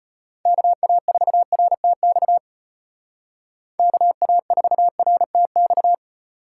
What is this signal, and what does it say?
Morse code, heard as a keyed tone, about 33 wpm, 700 Hz. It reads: KA4RTX KA4RTX